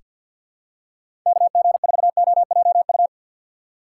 Morse code, heard as a keyed tone, about 48 wpm, 700 Hz. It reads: KG4OJU
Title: USA Call Signs - 6x Character Spacing - 50wpm